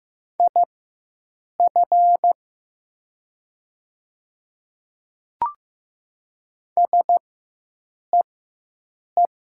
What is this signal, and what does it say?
Morse code, heard as a keyed tone, about 15 wpm, spaced as for 7 wpm, 700 Hz. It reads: IF SEE
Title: 4x Character Spacing - Top 200 Words - 15wpm